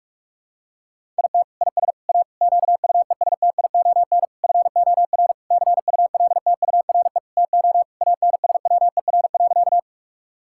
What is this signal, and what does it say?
Morse code, heard as a keyed tone, about 46 wpm, 700 Hz. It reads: IT IS A QUESTION FOR CULTURE TO ANSWER.
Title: Sentences from Top 1000 Words - Review Part 6 - 45wpm